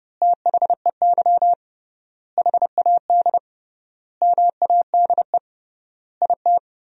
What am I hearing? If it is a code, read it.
Morse code, heard as a keyed tone, about 30 wpm, 700 Hz. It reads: THEY HAD MADE IT
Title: 3x Word-Spacing - Sentences from Top 500 Words - 30wpm